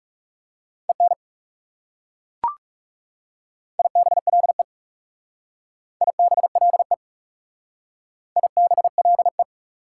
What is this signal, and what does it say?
Morse code, heard as a keyed tone, about 38 wpm, 690 Hz. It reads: EN IBLE IBLE IBLE